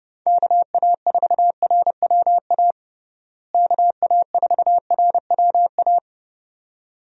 Morse code, heard as a keyed tone, about 30 wpm, 700 Hz. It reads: KA4RWA KA4RWA
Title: USA Call Signs - 6 Characters - Repeated 2x - 30wpm